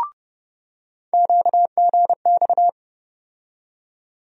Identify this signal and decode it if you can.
Morse code, heard as a keyed tone, about 30 wpm, 700 Hz. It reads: QGX
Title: USA Call Sign Postfixes - 3 Letters 30wpm